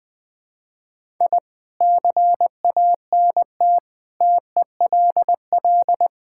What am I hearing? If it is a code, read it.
Morse code, heard as a keyed tone, about 21 wpm, 700 Hz. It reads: I CANT TELL
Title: Sentences with 7-Word Phrases - 20wpm